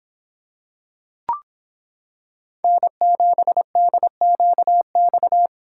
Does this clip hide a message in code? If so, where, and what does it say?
Yes: Morse code from 2.6 to 5.5 s, heard as a keyed tone, about 25 wpm, 700 Hz. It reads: N7DQX